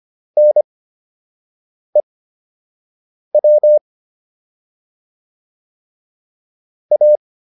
Morse code, heard as a keyed tone, about 25 wpm, spaced as for 6 wpm, 600 Hz. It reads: NEW A